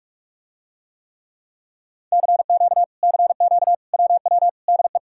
Morse code, heard as a keyed tone, about 46 wpm, 700 Hz. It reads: CQ CQ WW DE